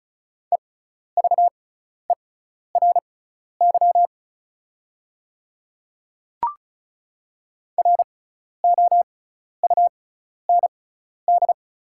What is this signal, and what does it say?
Morse code, heard as a keyed tone, about 36 wpm, spaced as for 12 wpm, 690 Hz. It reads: EVERY ROUND
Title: 6x Character Spacing - Top 300 Words - 35wpm